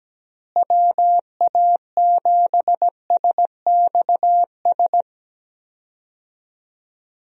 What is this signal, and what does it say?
Morse code, heard as a keyed tone, about 17 wpm, 700 Hz. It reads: WA7SXS
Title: USA Call Signs - 6 Characters - Repeated 3x - 17wpm